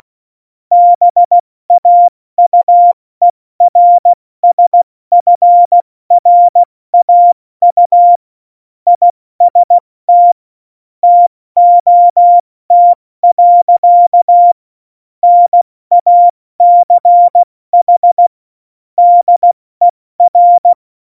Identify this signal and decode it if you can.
Morse code, heard as a keyed tone, about 16 wpm, 700 Hz. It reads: BAUERSFRAU IST TOT. NACH DER
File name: Morsekurs / Witze / Witz 21 Juli 2020, 70 F80